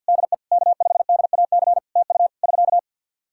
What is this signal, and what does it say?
Morse code, heard as a keyed tone, about 50 wpm, 700 Hz. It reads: DE K5DAC TU <SK>
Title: Practice QSO - Exchange 4 - Ending - 50wpm